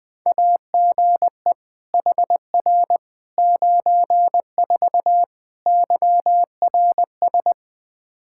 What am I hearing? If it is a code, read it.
Morse code, heard as a keyed tone, about 20 wpm, 700 Hz. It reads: AGE HR 94 YRS
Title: Practice QSO - Exchange 3 - More Info - 20wpm